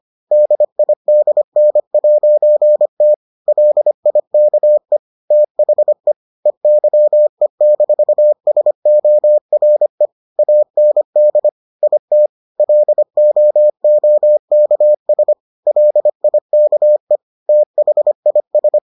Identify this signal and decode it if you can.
Morse code, heard as a keyed tone, about 25 wpm, 600 Hz. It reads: DIDN'T LIKE THE EYE-SORE AND IT LOOKS LIKE THIS